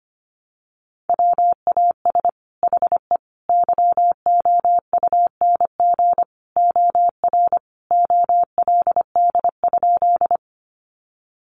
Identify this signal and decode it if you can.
Morse code, heard as a keyed tone, about 25 wpm, 700 Hz. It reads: WAS HE YOUNG OR OLD?
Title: Sentences from Top 400 Words 25wpm